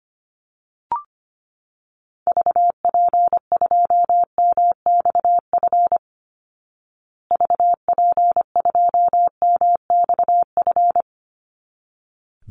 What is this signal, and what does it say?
Morse code, heard as a keyed tone, about 25 wpm, 700 Hz. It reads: VP2MXF VP2MXF
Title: Call Signs Repeated 3x - 25wpm